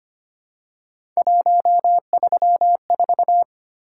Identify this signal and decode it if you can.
Morse code, heard as a keyed tone, about 25 wpm, 700 Hz. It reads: 134
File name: Three Letter-Number Combinations - 25wpm